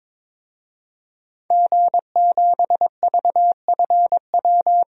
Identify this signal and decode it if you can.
Morse code, heard as a keyed tone, about 22 wpm, 700 Hz. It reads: G7VFW
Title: DXCC-100 Call Signs - 5 Characters - Repeated 3x - 22wpm